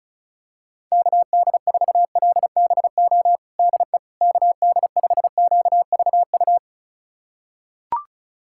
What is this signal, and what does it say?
Morse code, heard as a keyed tone, about 36 wpm, 700 Hz. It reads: KD4LBO DE KD5QVU
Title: USA Callsign DE Callsign - 6 Characters - 35wpm